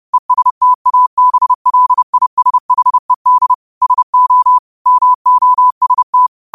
Morse code, heard as a keyed tone, about 30 wpm, 1000 Hz. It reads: ESTABLISHED SO MOST
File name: qotdfiles / 30wpm